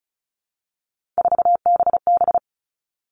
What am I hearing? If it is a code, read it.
Morse code, heard as a keyed tone, about 34 wpm, 700 Hz. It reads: 4BB
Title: All Cut Numbers - 3-Digit 35wpm